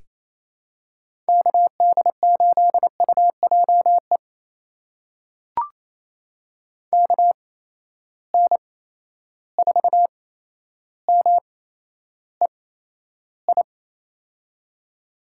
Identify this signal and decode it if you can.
Morse code, heard as a keyed tone, about 28 wpm, spaced as for 12 wpm, 700 Hz. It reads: KD8UJE KN4MEI
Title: USA Call Signs - 6 Characters - 8x Character Spacing - 28wpm